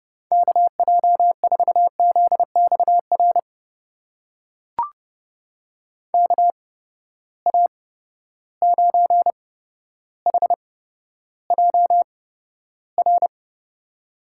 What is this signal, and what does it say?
Morse code, heard as a keyed tone, about 31 wpm, spaced as for 13 wpm, 700 Hz. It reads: KJ4ZXR KA9HJR
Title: USA Call Signs - 6 Characters - 8x Character Spacing - 30wpm